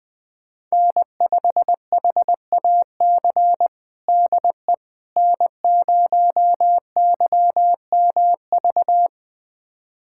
Morse code, heard as a keyed tone, about 20 wpm, 700 Hz. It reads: N5HAC DE N0YMV
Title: USA Callsign DE Callsign - 5 Characters - 20wpm